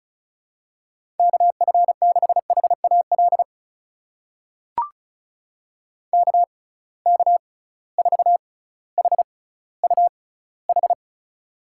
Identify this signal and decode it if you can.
Morse code, heard as a keyed tone, about 36 wpm, spaced as for 18 wpm, 700 Hz. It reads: KF6HAL KK4HUH
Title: USA Call Signs - 6x Character Spacing - 35wpm